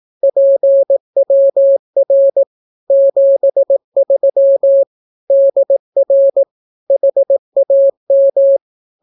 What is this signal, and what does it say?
Morse code, heard as a keyed tone, about 18 wpm, 550 Hz. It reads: PWR 73 DR HAM
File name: naukapodstaw / skroty / skroty